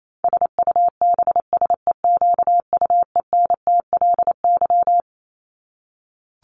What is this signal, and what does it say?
Morse code, heard as a keyed tone, about 28 wpm, 700 Hz. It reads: SUBSEQUENTLY
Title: LY Suffix - Top 100 Words with Base Words 28wpm